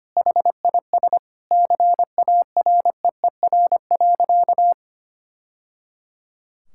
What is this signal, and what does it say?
Morse code, heard as a keyed tone, about 25 wpm, 700 Hz. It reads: HIS CAREER.
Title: Sentences from Top 1000 Words - Review Part 6 - 25wpm